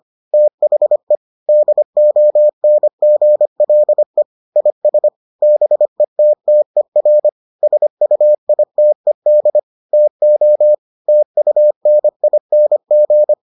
Morse code, heard as a keyed tone, about 25 wpm, 600 Hz. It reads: THE DONGLE IS BETTER SUITED TO TUNING